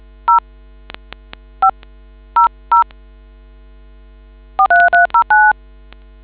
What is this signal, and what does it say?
DTMF, heard as telephone touch tones, tones of about 130 ms. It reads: 0 5 00 4309